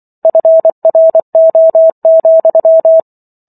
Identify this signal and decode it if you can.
Morse code, heard as a keyed tone, about 24 wpm, 650 Hz. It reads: FRO,